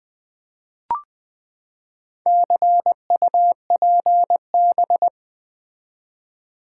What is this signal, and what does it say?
Morse code, heard as a keyed tone, about 20 wpm, 700 Hz. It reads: CUPB
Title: Build A Word - 4 Letters At A Time - 8-Letter Words - 20wpm